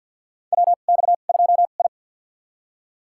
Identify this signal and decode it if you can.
Morse code, heard as a keyed tone, about 52 wpm, spaced as for 36 wpm, 700 Hz. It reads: WX2I
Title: USA Call Signs - 4 Characters - 2x Character Spacing - 50wpm